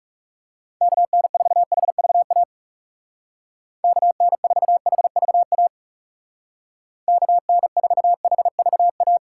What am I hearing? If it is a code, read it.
Morse code, heard as a keyed tone, about 36 wpm, 700 Hz. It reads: KN4HVA KN4HVA KN4HVA
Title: Speed Racing - USA 2x3 Call Signs 30wpm